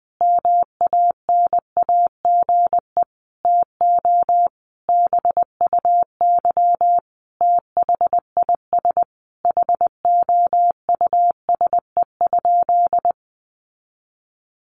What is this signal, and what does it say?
Morse code, heard as a keyed tone, about 20 wpm, 700 Hz. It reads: MANAGE TO BUY THIS HOUSE?
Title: Sentences from Top 1400 Words 20wpm